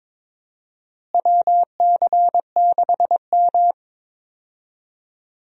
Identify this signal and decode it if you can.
Morse code, heard as a keyed tone, about 22 wpm, 700 Hz. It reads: WC6M
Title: USA Call Signs - 4 Characters - Repeated 2x - 22wpm